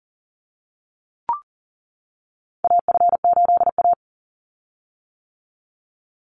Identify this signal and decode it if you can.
Morse code, heard as a keyed tone, about 40 wpm, 700 Hz. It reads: AF8A
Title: Call Signs - 40wpm